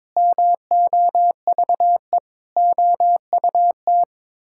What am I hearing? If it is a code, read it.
Morse code, heard as a keyed tone, about 22 wpm, 700 Hz. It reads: MOVE OUT
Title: Sentences from Top 1000 Words - Review Part 6 - 22wpm